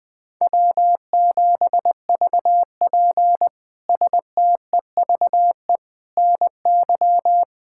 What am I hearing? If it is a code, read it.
Morse code, heard as a keyed tone, about 20 wpm, 700 Hz. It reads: W7VP STEVE NY
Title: K1USN SST QSO Practice - 20wpm